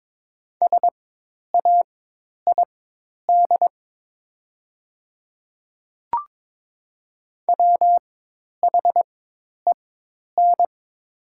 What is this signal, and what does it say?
Morse code, heard as a keyed tone, about 22 wpm, spaced as for 10 wpm, 700 Hz. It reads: SAID WHEN
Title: 4x Character Spacing - Top 300 Words - 22wpm